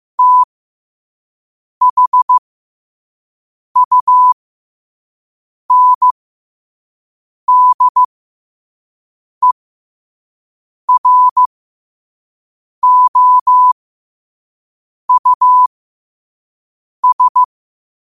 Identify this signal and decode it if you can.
Morse code, heard as a keyed tone, about 15 wpm, spaced as for 5 wpm, 1000 Hz. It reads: THUNDEROUS
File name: qotdfiles / 5wpm